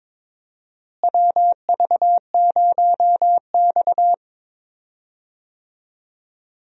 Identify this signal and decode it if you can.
Morse code, heard as a keyed tone, about 22 wpm, 700 Hz. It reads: WV0X
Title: USA Call Signs - 4 Characters - Repeated 2x - 22wpm